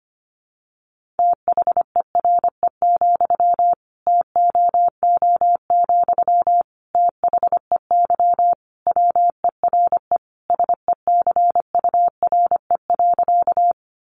Morse code, heard as a keyed tone, about 25 wpm, 700 Hz. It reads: THERE, TOO, THEY WERE SECURE.
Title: Sentences from Top 1200 Words 25wpm